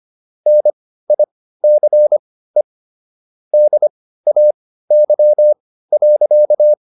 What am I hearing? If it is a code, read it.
Morse code, heard as a keyed tone, about 25 wpm, spaced as for 15 wpm, 600 Hz. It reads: NICE DAY.